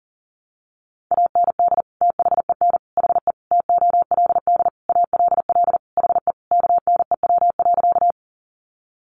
Morse code, heard as a keyed tone, about 40 wpm, 700 Hz. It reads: AND THEN HE TOLD ALL HE KNEW.
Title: Sentences from Top 400 Words 40wpm